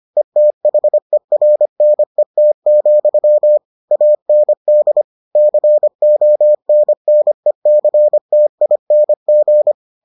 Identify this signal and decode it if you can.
Morse code, heard as a keyed tone, about 25 wpm, 600 Hz. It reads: ETHERNET, AND CONNECTING